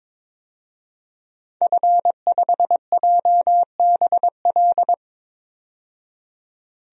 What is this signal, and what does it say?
Morse code, heard as a keyed tone, about 22 wpm, 700 Hz. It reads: F5JBL
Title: DXCC-100 Call Signs - 5 Characters - 22wpm